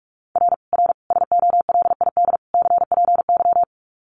Morse code, heard as a keyed tone, about 44 wpm, 700 Hz. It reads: R R SOLID CPY
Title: Common QSO Phrases 45wpm